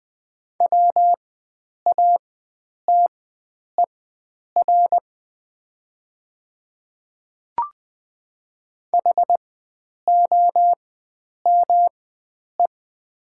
Morse code, heard as a keyed tone, about 20 wpm, spaced as for 9 wpm, 700 Hz. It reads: WATER HOME
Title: 4x Character Spacing - Top 300 Words - 20wpm